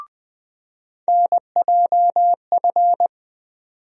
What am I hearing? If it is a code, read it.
Morse code, heard as a keyed tone, about 20 wpm, 700 Hz. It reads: NJF
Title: USA Call Sign Postfixes - 3 Letters 20wpm